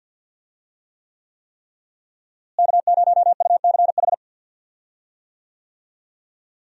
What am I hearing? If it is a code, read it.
Morse code, heard as a keyed tone, about 49 wpm, 700 Hz. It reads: K0UCH
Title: USA Call Signs - 5 Characters - Repeated 3x - 50wpm